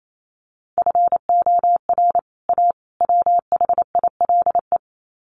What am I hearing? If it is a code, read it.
Morse code, heard as a keyed tone, about 28 wpm, 700 Hz. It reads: FOR A WHILE
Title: Sets of 3 Words 28wpm